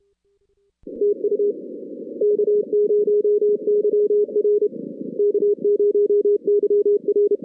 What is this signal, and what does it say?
Morse code, heard as a keyed tone, about 31 wpm, 410 Hz. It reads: TU K0YR K0YR
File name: 2016 arrl dx cw / 2016-ARRL-DX-CW-K5TR- - 0-22